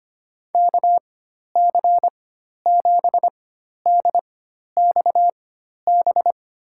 Morse code, heard as a keyed tone, about 25 wpm, spaced as for 12 wpm, 700 Hz. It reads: KC7DXB
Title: USA Call Signs - 6 Characters - 4x Character Spacing - 25wpm